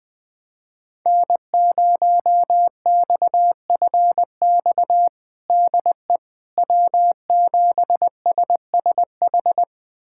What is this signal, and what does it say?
Morse code, heard as a keyed tone, about 20 wpm, 700 Hz. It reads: N0XFX DE W7SSH
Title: USA Callsign DE Callsign - 5 Characters - 20wpm